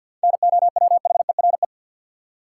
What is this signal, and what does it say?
Morse code, heard as a keyed tone, about 46 wpm, 700 Hz. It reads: NOWHERE
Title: Seven Letter Words - 50wpm